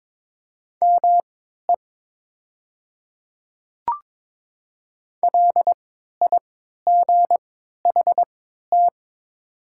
Morse code, heard as a keyed tone, about 22 wpm, spaced as for 12 wpm, 700 Hz. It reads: ME LIGHT
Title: 3x Character Spacing - Top 200 Words - 22wpm